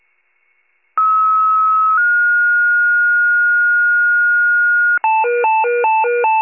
Two-tone page, heard as a one-second tone, then a three-second tone, about 1290 then 1430 Hz. It is heard from 1.0 to 5.0 s.
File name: pager / 84 FIRE / wcalert